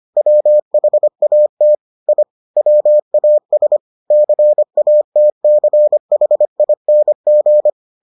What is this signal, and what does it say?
Morse code, heard as a keyed tone, about 25 wpm, 600 Hz. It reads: WHAT I WAS CATCHING